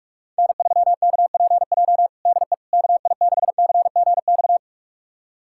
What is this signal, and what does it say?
Morse code, heard as a keyed tone, about 46 wpm, 700 Hz. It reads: N3KPJ DE KI6CGX
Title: USA Callsign DE Callsign 45wpm